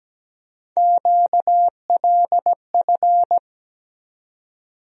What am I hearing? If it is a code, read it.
Morse code, heard as a keyed tone, about 17 wpm, 700 Hz. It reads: QLF
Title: USA Call Sign Postfixes - 3 Letters 17wpm